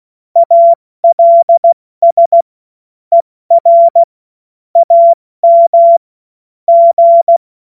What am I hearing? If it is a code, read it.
Morse code, heard as a keyed tone, about 16 wpm, 680 Hz. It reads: ALS ER AM G
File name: Morsekurs / Witze / Witz 31. Mai 2022 70 F80